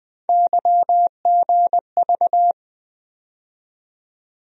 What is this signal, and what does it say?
Morse code, heard as a keyed tone, about 20 wpm, 700 Hz. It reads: YGV
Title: USA Call Sign Postfixes - 3 Letters 20wpm